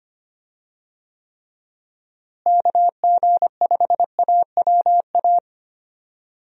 Morse code, heard as a keyed tone, about 25 wpm, 700 Hz. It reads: KG5AWA
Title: USA Call Signs - 6 Characters - Repeated 2x - 25wpm